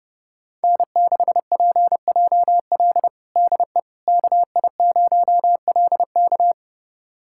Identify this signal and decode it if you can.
Morse code, heard as a keyed tone, about 32 wpm, 700 Hz. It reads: N6PJL DE KI0LK
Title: USA Callsign DE Callsign - 5 Characters - 30wpm